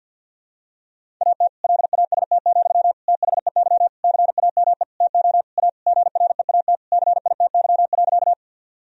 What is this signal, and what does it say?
Morse code, heard as a keyed tone, about 52 wpm, 700 Hz. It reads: AT LAST, THEY CAME TO A GREAT CITY.
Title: Sentences from Top 1000 Words - Review Part 1 - 50wpm